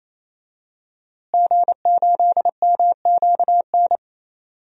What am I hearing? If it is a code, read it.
Morse code, heard as a keyed tone, about 27 wpm, 700 Hz. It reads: G8MQN
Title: DXCC-100 Call Signs - 5 Characters - Repeated 3x - 28wpm